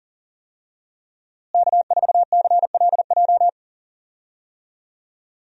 Morse code, heard as a keyed tone, about 37 wpm, 700 Hz. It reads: K4CLJ
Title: USA Call Signs - 5 Characters - Repeated 3x - 40wpm